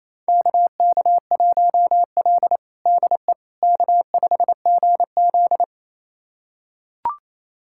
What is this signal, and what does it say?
Morse code, heard as a keyed tone, about 28 wpm, 700 Hz. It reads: KK1L DE K5GZ
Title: USA Callsign DE Callsign - 4 Characters - 28wpm